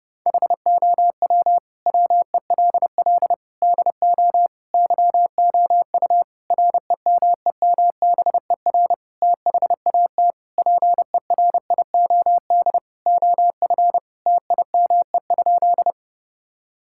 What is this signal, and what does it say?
Morse code, heard as a keyed tone, about 30 wpm, 700 Hz. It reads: HOW WELL DO YOU REMEMBER THAT PERIOD OF TIME?
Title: Sentences from Top 1000 Words - Review Part 5 - 30wpm